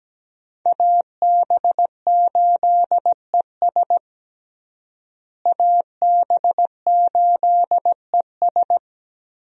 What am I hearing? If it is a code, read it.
Morse code, heard as a keyed tone, about 17 wpm, 700 Hz. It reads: AB8ES AB8ES
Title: USA Call Signs - 5 Characters - Repeated 2x - 17wpm